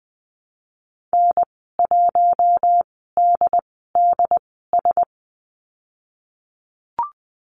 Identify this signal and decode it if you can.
Morse code, heard as a keyed tone, about 20 wpm, spaced as for 14 wpm, 700 Hz. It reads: N1DDS
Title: 076 Lesson 13g - Letter L - Call Signs 20wpm